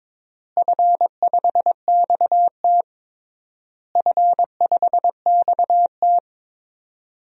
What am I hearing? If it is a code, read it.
Morse code, heard as a keyed tone, about 22 wpm, 700 Hz. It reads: F5XT F5XT
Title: DXCC-100 Call Signs - 4 Characters - Repeated 3x - 22wpm